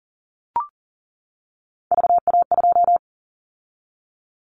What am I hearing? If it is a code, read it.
Morse code, heard as a keyed tone, about 40 wpm, 700 Hz. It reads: VA2
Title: Callsign Prefixes for Top 50 DXCC - 40wpm